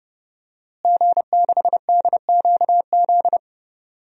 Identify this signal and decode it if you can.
Morse code, heard as a keyed tone, about 30 wpm, 700 Hz. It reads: G6DQZ
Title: DXCC-100 Call Signs - 5 Characters - Repeated 2x - 30wpm